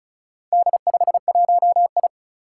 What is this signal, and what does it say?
Morse code, heard as a keyed tone, about 36 wpm, 700 Hz. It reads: D51I
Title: Four Challenging Letter-Number Combinations 35wpm